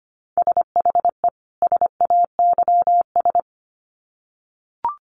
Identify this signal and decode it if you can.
Morse code, heard as a keyed tone, about 25 wpm, 700 Hz. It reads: SHE SAYS
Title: Sets of 2 Words 25wpm